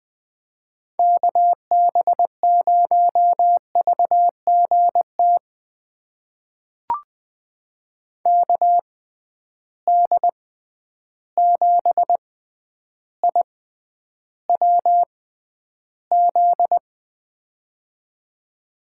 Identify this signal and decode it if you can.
Morse code, heard as a keyed tone, about 20 wpm, spaced as for 10 wpm, 700 Hz. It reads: KB0VGT KD7IWZ
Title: USA Call Signs - 6 Characters - 6x Character Spacing - 20wpm